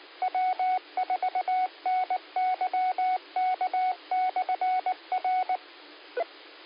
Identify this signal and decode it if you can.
Morse code, heard as a keyed tone, about 19 wpm, 740 Hz, with radio noise. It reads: W4NYK/R